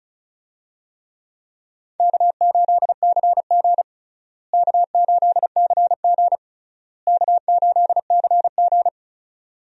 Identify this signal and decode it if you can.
Morse code, heard as a keyed tone, about 34 wpm, 700 Hz. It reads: K8CG K8CG K8CG
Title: USA Call Signs - 4 Characters - Repeated 3x - 35wpm